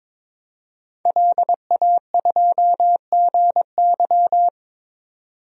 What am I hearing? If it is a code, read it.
Morse code, heard as a keyed tone, about 22 wpm, 700 Hz. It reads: LA2GY
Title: DXCC-100 Call Signs - 5 Characters - 22wpm